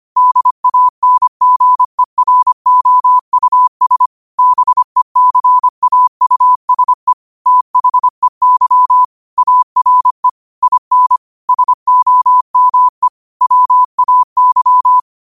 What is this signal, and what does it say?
Morse code, heard as a keyed tone, about 25 wpm, 1000 Hz. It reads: DANGEROUS BECAUSE THEY ARE IN SOME WAY
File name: 25wpm